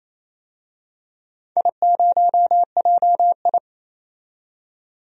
Morse code, heard as a keyed tone, about 28 wpm, 700 Hz. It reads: I0JI
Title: DXCC-100 Call Signs - 4 Characters - 28wpm